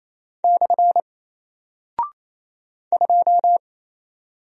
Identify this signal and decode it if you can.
Morse code, heard as a keyed tone, about 27 wpm, 700 Hz. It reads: / 2
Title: Single Character (Letter:Number:Punctuation:Prosign) 28wpm